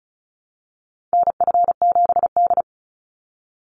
Morse code, heard as a keyed tone, about 36 wpm, 700 Hz. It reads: NF7D
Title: Call Signs - 35wpm